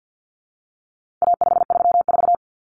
Morse code, heard as a keyed tone, about 50 wpm, 700 Hz. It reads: A534
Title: Common Cut Numbers - 4-Digit 50wpm